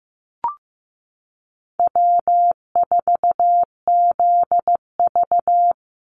Morse code, heard as a keyed tone, about 15 wpm, 700 Hz. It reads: W4ZV
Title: NA SOTA Chaser Callsigns 15wpm